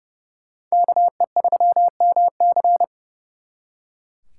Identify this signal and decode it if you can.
Morse code, heard as a keyed tone, about 37 wpm, 700 Hz. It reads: KE3MC
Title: Speed Racing - USA 2x2 Call Signs 30wpm